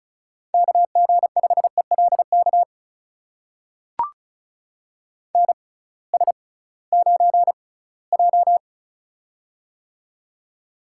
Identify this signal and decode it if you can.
Morse code, heard as a keyed tone, about 36 wpm, spaced as for 20 wpm, 700 Hz. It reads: KG5ELK NS9J
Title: USA Call Signs - 6x Character Spacing - 35wpm